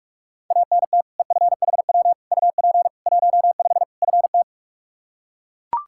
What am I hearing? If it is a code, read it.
Morse code, heard as a keyed tone, about 41 wpm, 700 Hz. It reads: ANT EFHW UP 15 FT
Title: Practice QSO - Exchange 2 - Info - 45wpm